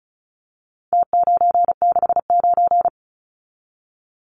Morse code, heard as a keyed tone, about 34 wpm, 700 Hz. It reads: T969
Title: Numbers - Contest Style - 4-Digit 35wpm